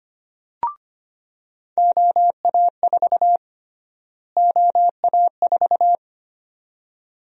Morse code, heard as a keyed tone, about 25 wpm, 700 Hz. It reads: OA4 OA4
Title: USA Call Sign Prefixes - Repeated 2x - 25wpm